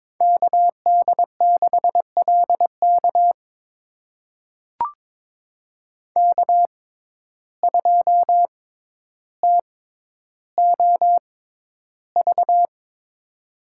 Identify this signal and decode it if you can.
Morse code, heard as a keyed tone, about 21 wpm, spaced as for 11 wpm, 700 Hz. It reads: KD6LK K2TOV
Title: USA Call Signs - 5 Characters - 6x Character Spacing - 22wpm